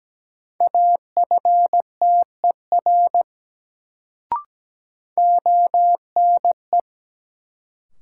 Morse code, heard as a keyed tone, about 17 wpm, 700 Hz. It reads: AFTER ONE
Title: Sets of 2 Words - New Word Drill - 17wpm